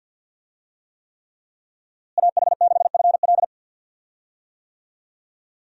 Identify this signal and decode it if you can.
Morse code, heard as a keyed tone, about 53 wpm, 700 Hz. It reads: AH6FL
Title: USA Call Signs - 5 Characters - Repeated 3x - 50wpm